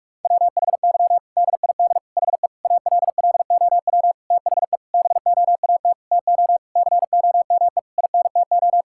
Morse code, heard as a keyed tone, about 46 wpm, 700 Hz. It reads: WHY DID HE ALLOW THE BOAT TO COME INTO
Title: Sentences from Top 1000 Words - Review Part 5 - 45wpm